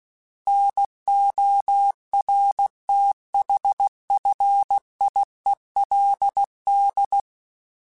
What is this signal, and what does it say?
Morse code, heard as a keyed tone, about 16 wpm, 800 Hz. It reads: NORTHFIELD
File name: fromSMs / FromSMs mp3s / Cities@15